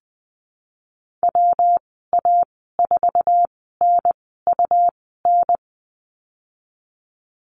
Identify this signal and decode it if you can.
Morse code, heard as a keyed tone, about 20 wpm, spaced as for 15 wpm, 700 Hz. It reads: WA4NUN
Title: 186 Lesson 30g - Letter b - Call Signs 20wpm